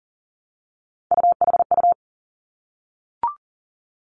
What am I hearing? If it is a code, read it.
Morse code, heard as a keyed tone, about 40 wpm, 700 Hz. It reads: UHU
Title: Three Challenging Letter-Number Combinations 40wpm